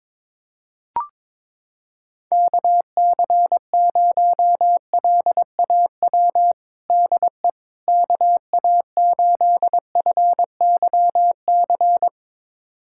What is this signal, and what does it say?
Morse code, heard as a keyed tone, about 22 wpm, 700 Hz. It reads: KC0LAW DE KA8FYC
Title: USA Callsign DE Callsign - 6 Characters - 22wpm